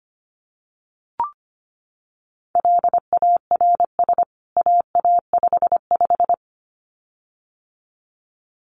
Morse code, heard as a keyed tone, about 25 wpm, 700 Hz. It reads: LARS AA55
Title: CWT Exchange 25wpm